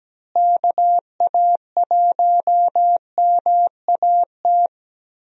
Morse code, heard as a keyed tone, about 17 wpm, 700 Hz. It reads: KA1MAT
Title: USA Call Signs - 6 Characters - Repeated 3x - 17wpm